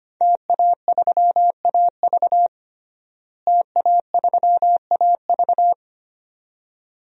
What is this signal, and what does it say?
Morse code, heard as a keyed tone, about 25 wpm, 700 Hz. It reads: TA3AV TA3AV
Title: DXCC-100 Call Signs - 5 Characters - Repeated 2x - 25wpm